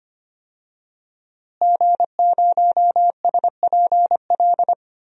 Morse code, heard as a keyed tone, about 25 wpm, 700 Hz. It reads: G0SPL
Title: DXCC-100 Call Signs - 5 Characters - Repeated 3x - 25wpm